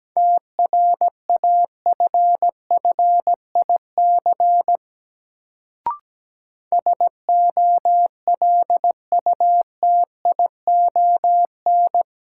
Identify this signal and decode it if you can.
Morse code, heard as a keyed tone, about 17 wpm, 700 Hz. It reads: TRAFFIC SOLUTION
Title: Sentences from Top 1300 Words - New Word Drill - 17wpm